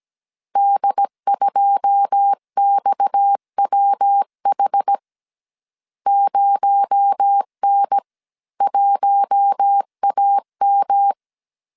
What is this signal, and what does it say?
Morse code, heard as a keyed tone, about 17 wpm, 800 Hz. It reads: D2XWH 0N1AM
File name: CW / DET / 1 9 1500C S C 16wpm 2909